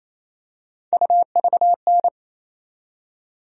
Morse code, heard as a keyed tone, about 28 wpm, 700 Hz. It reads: UVN
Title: USA Call Sign Postfixes - 3 Letters 28wpm